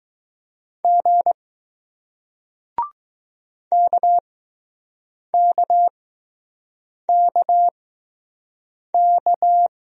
Morse code, heard as a keyed tone, about 20 wpm, spaced as for 6 wpm, 700 Hz. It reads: G KKKK